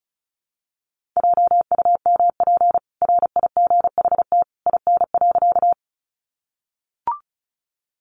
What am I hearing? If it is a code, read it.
Morse code, heard as a keyed tone, about 34 wpm, 700 Hz. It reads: JUMP RIGHT IN.
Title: Sentences from Top 1300 Words 35wpm